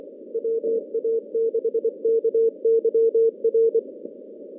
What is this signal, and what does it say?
Morse code, heard as a keyed tone, about 27 wpm, 440 Hz. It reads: WA6KYR
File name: contests / CW-SS-k5tr- -2100